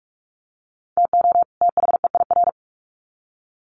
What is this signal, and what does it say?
Morse code, heard as a keyed tone, about 44 wpm, 700 Hz. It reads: TO THEIR
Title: Sets of 2 Words 45wpm